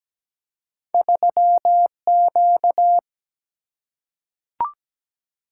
Morse code, heard as a keyed tone, about 17 wpm, 700 Hz. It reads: 3Q
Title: Single-Letter Single-Number 17wpm